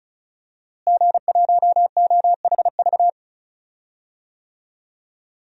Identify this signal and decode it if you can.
Morse code, heard as a keyed tone, about 34 wpm, 700 Hz. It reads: G1OHV
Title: DXCC-100 Call Signs - 5 Characters - Repeated 2x - 35wpm